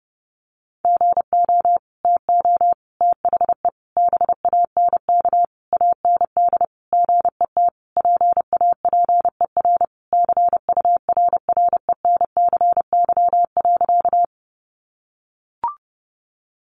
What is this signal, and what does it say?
Morse code, heard as a keyed tone, about 30 wpm, 700 Hz. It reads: GO TO THE BANK AND GET PAPER CURRENCY.
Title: Sentences from Top 1500 Words 30wpm